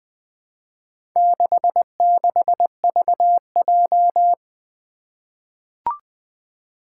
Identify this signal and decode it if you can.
Morse code, heard as a keyed tone, about 20 wpm, 700 Hz. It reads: 66VJ
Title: Four Challenging Letter-Number Combinations 20wpm